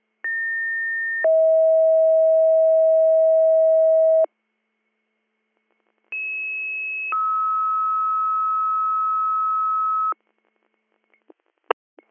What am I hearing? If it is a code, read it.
Two-tone page, heard as a one-second tone, then a three-second tone, about 1800 then 640 Hz, about 2600 then 1290 Hz. It is heard from 0.2 to 10.1 s.